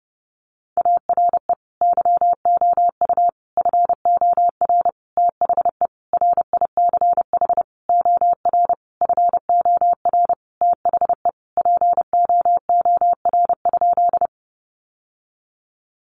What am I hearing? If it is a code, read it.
Morse code, heard as a keyed tone, about 30 wpm, 700 Hz. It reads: ARE YOU FOR THE RICH OR FOR THE POOR?
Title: Sentences from Top 1100 Words 30wpm